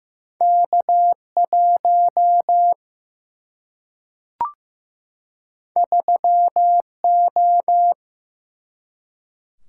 Morse code, heard as a keyed tone, about 15 wpm, 700 Hz. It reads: K 1 3 O
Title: Single-Letter Single-Number 15wpm